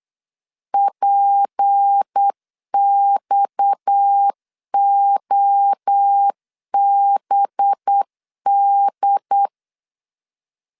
Morse code, heard as a keyed tone, about 8 wpm, 800 Hz. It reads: PXOBD
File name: LEZIONI / 6 PKX0 / 6 PKX 8wpm 3041